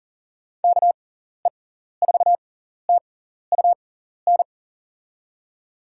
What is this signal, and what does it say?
Morse code, heard as a keyed tone, about 40 wpm, spaced as for 14 wpm, 700 Hz. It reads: KE4TUN
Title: USA Call Signs - 6 Characters - 6x Character Spacing - 40wpm